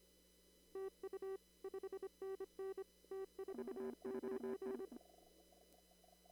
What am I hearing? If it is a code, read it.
Morse code, heard as a keyed tone, about 25 wpm, 380 Hz. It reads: TU 5NN T49
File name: CW / kt5j wpx cw-op-k5pi